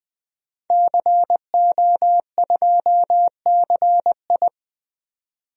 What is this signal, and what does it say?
Morse code, heard as a keyed tone, about 20 wpm, 700 Hz. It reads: CO2CI
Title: DXCC-100 Call Signs - 5 Characters - 20wpm